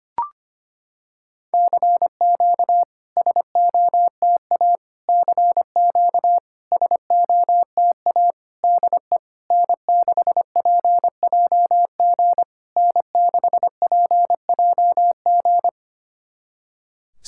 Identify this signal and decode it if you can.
Morse code, heard as a keyed tone, about 25 wpm, 690 Hz. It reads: CQ SOTA CQ SOTA DE N6PJG N6PJG
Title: SOTA QSO Practice 25wpm